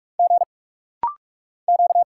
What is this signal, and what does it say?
Morse code, heard as a keyed tone, about 46 wpm, 700 Hz. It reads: G Q
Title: Single Letters - Rapid-Fire 45wpm